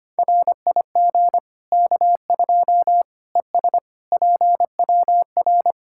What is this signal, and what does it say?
Morse code, heard as a keyed tone, about 25 wpm, 700 Hz. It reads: RIG K2 ES PWR
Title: Practice QSO - Exchange 2 - Info - 25wpm